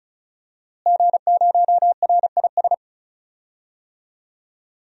Morse code, heard as a keyed tone, about 34 wpm, 690 Hz. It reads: G0RIS
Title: DXCC-100 Call Signs - 5 Characters - Repeated 3x - 35wpm